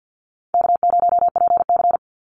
Morse code, heard as a keyed tone, about 50 wpm, 700 Hz. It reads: K0PC
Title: Easy Call Signs Repeated 3x - 50wpm